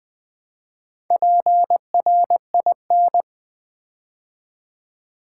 Morse code, heard as a keyed tone, about 20 wpm, 700 Hz. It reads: PRIN
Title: Build A Word - 2 Letters At A Time - 8-Letter Words - 20wpm